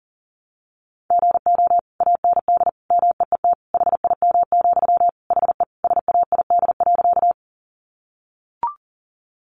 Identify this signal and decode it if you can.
Morse code, heard as a keyed tone, about 40 wpm, 700 Hz. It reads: GO AND MEET HIM, HE SAID.
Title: Sentences from Top 1100 Words 40wpm